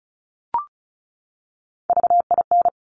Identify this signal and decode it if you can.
Morse code, heard as a keyed tone, about 36 wpm, 700 Hz. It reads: VIN
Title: Top Three Letter Combinations in Words 35wpm